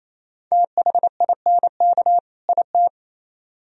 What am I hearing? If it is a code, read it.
Morse code, heard as a keyed tone, about 35 wpm, 700 Hz. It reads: THINK IT
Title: Speed-Racing - Sets of 2 Words 28wpm